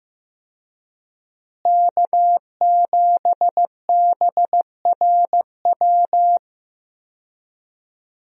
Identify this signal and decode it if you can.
Morse code, heard as a keyed tone, about 15 wpm, 700 Hz. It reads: K7BRW
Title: USA Call Signs - 5 Characters - Repeated 3x - 15wpm